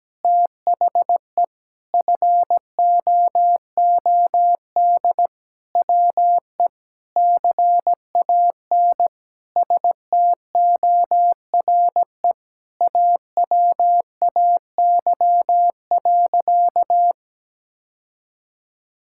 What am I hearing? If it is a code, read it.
Morse code, heard as a keyed tone, about 17 wpm, 700 Hz. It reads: THE FOOD WE CAN STORE AWAY.
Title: Sentences from Top 1000 Words - Review Part 5 - 17wpm